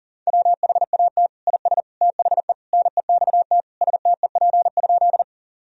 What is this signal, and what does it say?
Morse code, heard as a keyed tone, about 40 wpm, 700 Hz. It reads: WHAT IS THE NEXT STEP?
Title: Sentences from Top 1000 Words - Review Part 2 - 40wpm